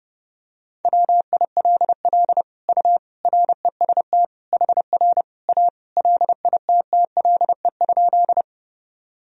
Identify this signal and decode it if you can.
Morse code, heard as a keyed tone, about 30 wpm, 700 Hz. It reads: WILL U REST HR A LITTLE?
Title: Sentences from Top 500 Words and Abbreviations 30wpm